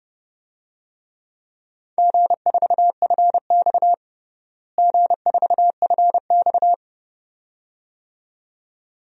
Morse code, heard as a keyed tone, about 30 wpm, 700 Hz. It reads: G4FX G4FX
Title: DXCC-100 Call Signs - 4 Characters - Repeated 2x - 30wpm